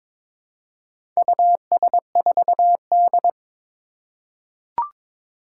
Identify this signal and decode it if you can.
Morse code, heard as a keyed tone, about 22 wpm, 700 Hz. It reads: US4D
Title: Four Challenging Letter-Number Combinations 22wpm